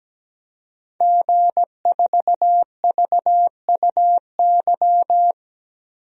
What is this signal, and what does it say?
Morse code, heard as a keyed tone, about 17 wpm, 700 Hz. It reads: G4VUY
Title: DXCC-100 Call Signs - 5 Characters - 17wpm